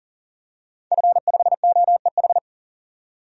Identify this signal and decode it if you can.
Morse code, heard as a keyed tone, about 38 wpm, 700 Hz. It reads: F5OEH
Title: DXCC-100 Call Signs - 5 Characters - 40wpm